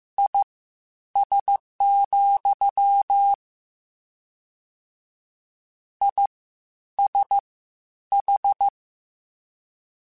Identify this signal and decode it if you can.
Morse code, heard as a keyed tone, about 15 wpm, spaced as for 9 wpm, 800 Hz. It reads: IS, ISH
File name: General / General-mp3s / Suffixes sp @13